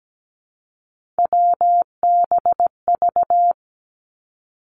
Morse code, heard as a keyed tone, about 17 wpm, 700 Hz. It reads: WBV
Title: Three Challenging Letter-Number Combinations 17wpm